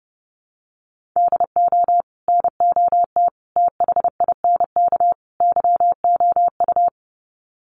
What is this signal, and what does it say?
Morse code, heard as a keyed tone, about 30 wpm, 700 Hz. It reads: DO NOT THINK YOU
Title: Sets of 4 Words 30wpm